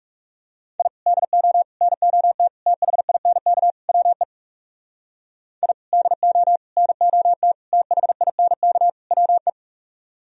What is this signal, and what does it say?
Morse code, heard as a keyed tone, about 40 wpm, 700 Hz. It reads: I DO NOT THINK WE I DO NOT THINK WE